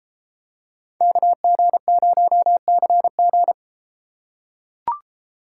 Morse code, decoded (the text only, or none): KG0CG